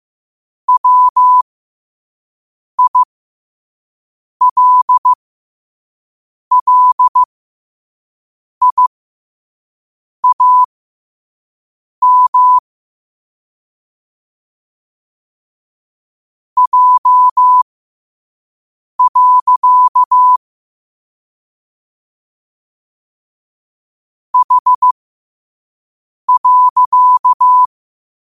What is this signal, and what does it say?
Morse code, heard as a keyed tone, about 15 wpm, spaced as for 5 wpm, 1000 Hz. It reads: WILLIAM J. H.
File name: qotdfiles / 5wpm